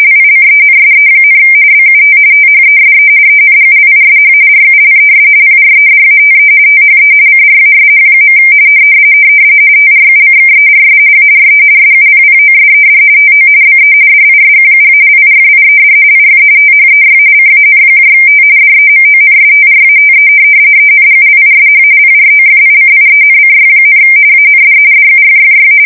RTTY, about 45.45 baud. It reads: LICENSE, BUT THESE ENGLISH RULES PREDATE THE CRASH. TRUCKERS ARE SUPPOSED TO BE DISQUALIFIED IF THEY CANT DEMONSTRATE ENGLISH PROFICIENCY, AND DUFFY
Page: 170 Shift 2125/2295